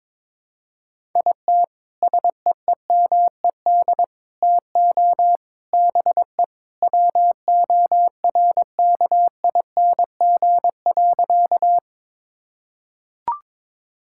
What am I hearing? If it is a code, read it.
Morse code, heard as a keyed tone, about 22 wpm, 700 Hz. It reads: IT SEEMED TO BE WORKING.
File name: practice / 22wpm / Sentences with 3-Word Phrases - 22wpm